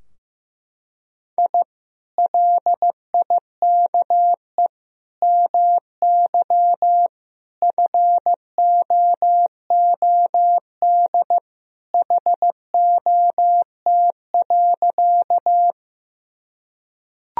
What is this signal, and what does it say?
Morse code, heard as a keyed tone, about 16 wpm, 700 Hz. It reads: I LIKE MY FOOD HOT.
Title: Sentences from Top 1000 Words - Review Part 3 - 15wpm